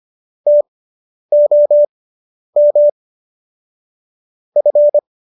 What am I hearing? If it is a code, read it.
Morse code, heard as a keyed tone, about 25 wpm, spaced as for 10 wpm, 590 Hz. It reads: TOM F